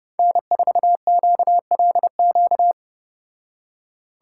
Morse code, heard as a keyed tone, about 34 wpm, 700 Hz. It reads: N4QLQ